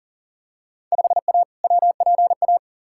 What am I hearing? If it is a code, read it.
Morse code, heard as a keyed tone, about 40 wpm, 700 Hz. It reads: 5A WPA
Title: Field Day Exchanges Encore - 40wpm